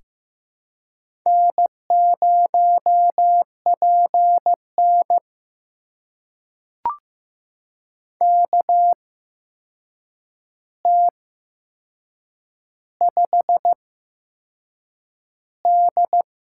Morse code, heard as a keyed tone, about 15 wpm, spaced as for 6 wpm, 700 Hz. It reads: N0PN KT5D
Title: USA Call Signs - 4 Characters - 8x Character Spacing - 15wpm